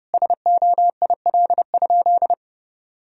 Morse code, heard as a keyed tone, about 30 wpm, 700 Hz. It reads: SOIL?
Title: Sentences from Top 1000 Words - Review Part 4 - 30wpm